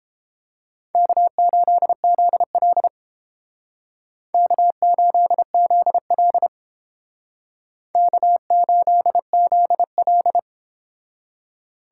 Morse code, heard as a keyed tone, about 30 wpm, 700 Hz. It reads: K8ZL K8ZL K8ZL